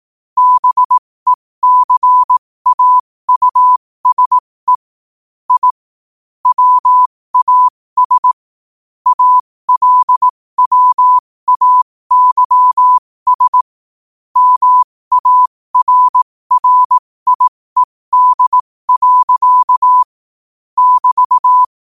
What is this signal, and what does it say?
Morse code, heard as a keyed tone, about 18 wpm, spaced as for 15 wpm, 1000 Hz. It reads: BECAUSE I WAS ALWAYS MARRIED. <BT>
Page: 15 WPM morse code quotes